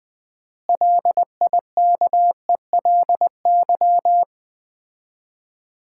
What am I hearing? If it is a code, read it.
Morse code, heard as a keyed tone, about 21 wpm, 700 Hz. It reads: LIKELY
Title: 4x Word-Spacing - Sets of 4 Words - 20wpm